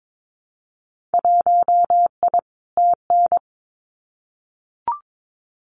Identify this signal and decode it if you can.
Morse code, heard as a keyed tone, about 22 wpm, 700 Hz. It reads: 1I TN
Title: Winter Field Day Exchanges Encore 22wpm